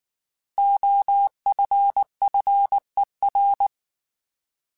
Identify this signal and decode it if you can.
Morse code, heard as a keyed tone, about 19 wpm, 800 Hz. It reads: OFFER
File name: Alphabetical / Alphabetical mp3s / Os @18